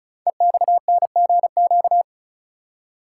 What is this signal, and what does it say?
Morse code, heard as a keyed tone, about 36 wpm, 700 Hz. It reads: EXNGQ
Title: Code Groups - 5 Letters - 35wpm